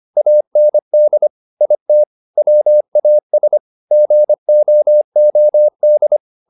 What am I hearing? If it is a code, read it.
Morse code, heard as a keyed tone, about 25 wpm, 600 Hz. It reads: AND IT WAS GOOD
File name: the old man and the sea part 1 25wpm